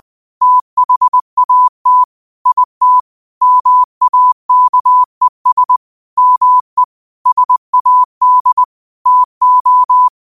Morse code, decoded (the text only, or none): THAT IT MAKES ME SAD TO